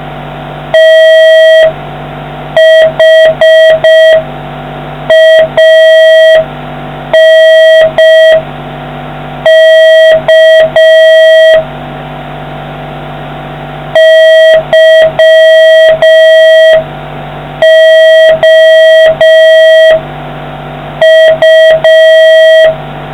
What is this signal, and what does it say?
Morse code, heard as a keyed tone, about 5 wpm, 1950 Hz, with noise. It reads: THANK YOU